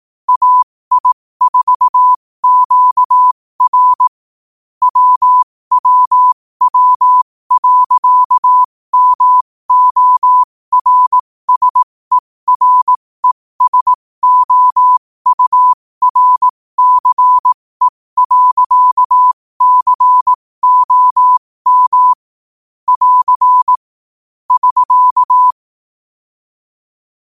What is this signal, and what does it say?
Morse code, heard as a keyed tone, about 18 wpm, spaced as for 15 wpm, 1000 Hz. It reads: AI4QR WWW.MORSERESOURCE.COM <AR> <SK>
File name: qotdfiles / 15wpm